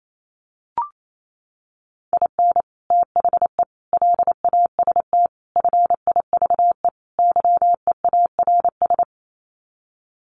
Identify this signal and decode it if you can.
Morse code, heard as a keyed tone, about 28 wpm, 700 Hz. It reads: IN THE LAST FIVE YEARS
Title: Sets of 5 Words Encore 28wpm